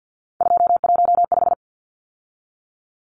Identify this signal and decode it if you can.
Morse code, heard as a keyed tone, about 50 wpm, 700 Hz. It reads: 215
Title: Numbers - 3-Digit 50wpm